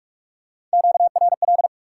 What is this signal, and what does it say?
Morse code, heard as a keyed tone, about 46 wpm, 700 Hz. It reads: QRL
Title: Top 90 QSO Elements 45wpm